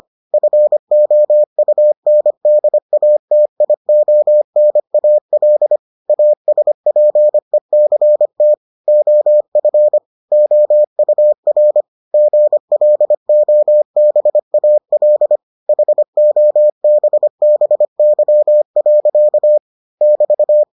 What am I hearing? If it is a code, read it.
Morse code, heard as a keyed tone, about 25 wpm, 600 Hz. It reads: FOUNDATIONAL ASPECT OF OUR GLOBAL HOBBY. <BT>